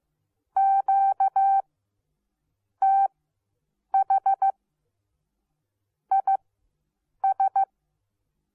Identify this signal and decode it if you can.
Morse code, heard as a keyed tone, about 15 wpm, spaced as for 7 wpm, 780 Hz. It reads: QTH IS